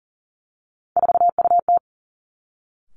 Morse code, heard as a keyed tone, about 40 wpm, 700 Hz. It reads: 4UT
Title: Three Challenging Letter-Number Combinations 40wpm